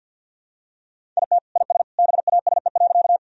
Morse code, heard as a keyed tone, about 52 wpm, 700 Hz. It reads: IT IS BASE.
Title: Sentences from Top 1000 Words - Review Part 1 - 50wpm